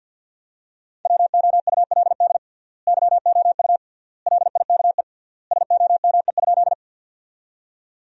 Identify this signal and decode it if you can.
Morse code, heard as a keyed tone, about 50 wpm, 700 Hz. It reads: WOULD YOU LIKE SOME?
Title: 3x Word-Spacing - Sentences from Top 100 Words - 50wpm